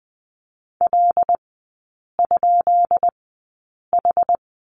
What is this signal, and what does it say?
Morse code, heard as a keyed tone, about 20 wpm, spaced as for 8 wpm, 700 Hz. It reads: L?H
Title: 152 Lesson 25e - Letter p - Three Characters 20wpm